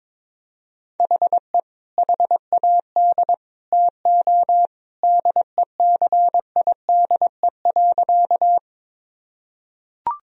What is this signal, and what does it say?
Morse code, heard as a keyed tone, about 21 wpm, 700 Hz. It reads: HE HAD TO DECIDE.
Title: Sentences from Top 1000 Words - Review Part 3 - 22wpm